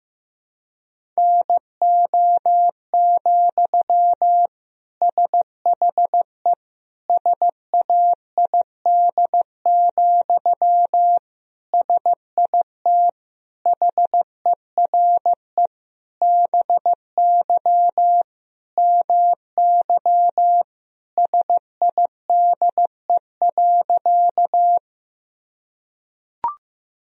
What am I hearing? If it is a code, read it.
Morse code, heard as a keyed tone, about 15 wpm, 700 Hz. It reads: NO, SHE SAID, SIT HERE BY MY SIDE.
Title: Sentences from Top 1000 Words - Review Part 1 - 15wpm